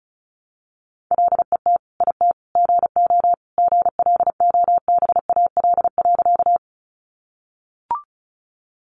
Morse code, heard as a keyed tone, about 34 wpm, 700 Hz. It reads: LET IT GO GLOBAL.
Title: Sentences from Top 1200 Words 35wpm